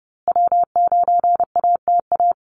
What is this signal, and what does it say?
Morse code, heard as a keyed tone, about 29 wpm, 700 Hz. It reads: W9ATA
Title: USA Call Signs Repeated 3x 30wpm